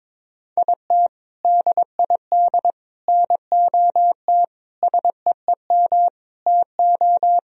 Morse code, heard as a keyed tone, about 22 wpm, 700 Hz. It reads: IT DID NOT SEEM TO
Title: Sentences from Top 1000 Words - Review Part 1 - 22wpm